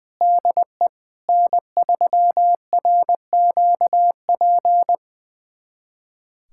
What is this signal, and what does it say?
Morse code, heard as a keyed tone, about 21 wpm, 700 Hz. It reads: DE N3RQP
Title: USA Callsign DE Callsign - 5 Characters - 20wpm